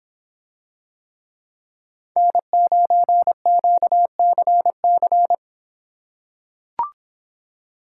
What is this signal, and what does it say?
Morse code, heard as a keyed tone, about 26 wpm, 700 Hz. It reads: N9QCC